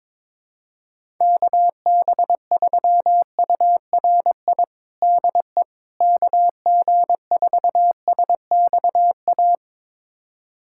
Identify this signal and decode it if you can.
Morse code, heard as a keyed tone, about 22 wpm, 700 Hz. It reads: KB3URI DE KG4SXA
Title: USA Callsign DE Callsign 22wpm